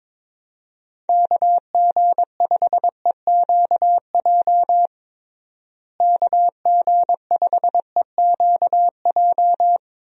Morse code, heard as a keyed tone, about 22 wpm, 700 Hz. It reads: KG5EQJ KG5EQJ
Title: USA Call Signs - 6 Characters - Repeated 3x - 22wpm